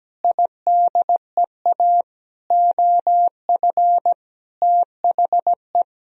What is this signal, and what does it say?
Morse code, heard as a keyed tone, about 17 wpm, 700 Hz. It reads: IDEA OF THE
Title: Sentences from Top 1000 Words - Review Part 3 - 17wpm